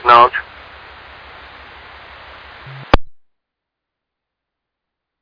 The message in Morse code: none

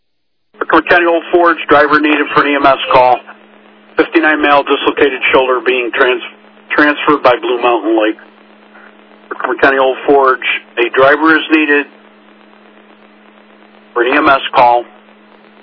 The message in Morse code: none

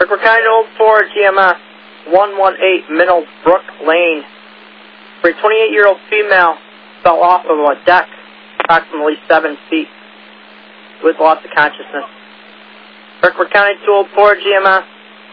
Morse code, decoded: none

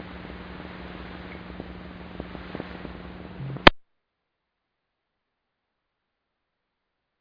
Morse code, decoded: none

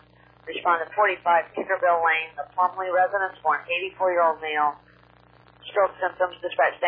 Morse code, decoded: none

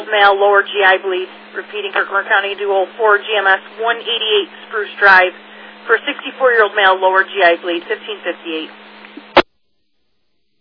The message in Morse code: none

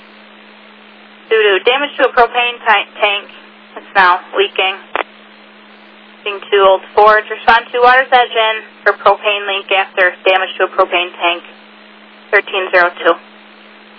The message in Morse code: none

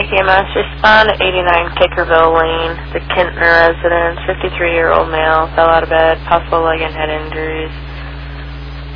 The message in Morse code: none

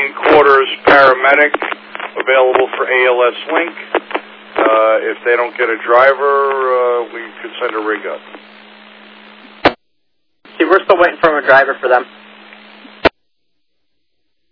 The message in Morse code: none

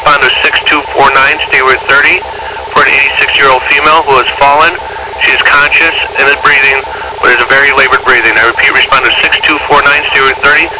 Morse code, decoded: none

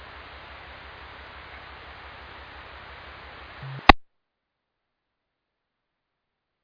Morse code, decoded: none

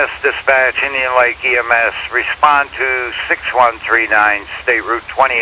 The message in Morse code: none

